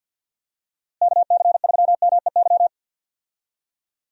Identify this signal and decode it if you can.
Morse code, heard as a keyed tone, about 46 wpm, 700 Hz. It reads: KC3MEY
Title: USA Call Signs - 6 Characters - Repeated 3x - 50wpm